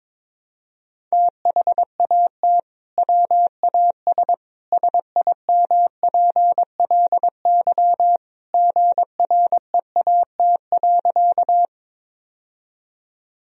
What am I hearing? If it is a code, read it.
Morse code, heard as a keyed tone, about 22 wpm, 700 Hz. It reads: THAT WAS SIMPLY GREAT.
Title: Sentences from Top 1000 Words - Review Part 4 - 22wpm